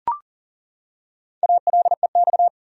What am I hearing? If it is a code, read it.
Morse code, heard as a keyed tone, about 39 wpm, 700 Hz. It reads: ALEX
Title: USA Common Names - Max 4 Letters - 40wpm